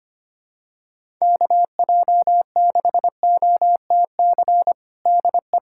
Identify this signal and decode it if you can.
Morse code, heard as a keyed tone, about 25 wpm, 700 Hz. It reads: KJ6OTC DE
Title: USA Callsign DE Callsign 25wpm